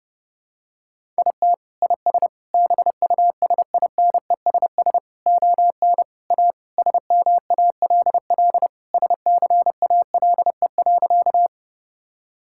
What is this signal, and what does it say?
Morse code, heard as a keyed tone, about 31 wpm, 700 Hz. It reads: IT IS BUSINESS ON A SMALL SCALE.
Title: Sentences from Top 1000 Words - Review Part 5 - 30wpm